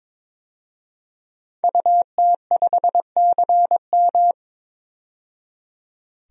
Morse code, heard as a keyed tone, about 22 wpm, 700 Hz. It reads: UT5CM
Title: DXCC-100 Call Signs - 5 Characters - 22wpm